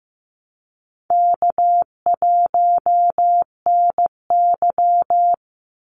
Morse code, heard as a keyed tone, about 15 wpm, 700 Hz. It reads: K1NY
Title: Easy Call Signs Repeated 2x - 15wpm